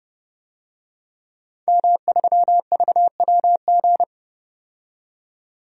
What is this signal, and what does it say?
Morse code, heard as a keyed tone, about 30 wpm, 700 Hz. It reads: M3VWG
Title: DXCC-100 Call Signs - 5 Characters - 30wpm